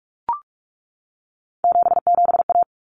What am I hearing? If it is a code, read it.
Morse code, heard as a keyed tone, about 44 wpm, 700 Hz. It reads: 77A
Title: Common Cut Numbers - 3-Digit 45wpm